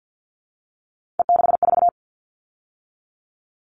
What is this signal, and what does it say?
Morse code, heard as a keyed tone, about 50 wpm, 700 Hz. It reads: E64
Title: All Cut Numbers - 3-Digit 50wpm